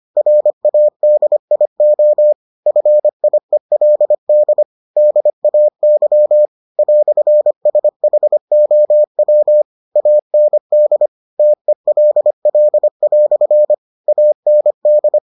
Morse code, heard as a keyed tone, about 25 wpm, 600 Hz. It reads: RADIO FIELD DAY "SHOW AND TELL" AND